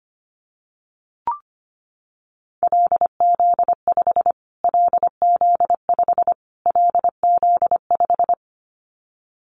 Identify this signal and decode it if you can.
Morse code, heard as a keyed tone, about 25 wpm, 700 Hz. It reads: LZ5 LZ5 LZ5
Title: Callsign Prefixes for Top 25 DXCC - Repeated 3x - 25wpm